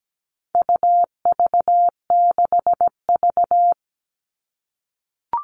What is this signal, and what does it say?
Morse code, heard as a keyed tone, about 17 wpm, 700 Hz. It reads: UV6V
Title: All Cut Numbers - 4-Digit 17wpm